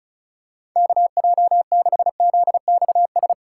Morse code, heard as a keyed tone, about 34 wpm, 700 Hz. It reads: KJ6ZXS
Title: USA Call Signs - 6 Characters - 2x Character Spacing - 35wpm